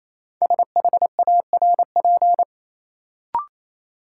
Morse code, heard as a keyed tone, about 28 wpm, 700 Hz. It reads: SHARP
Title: Sentences from Top 1600 Words - New Word Drill Alt - 28wpm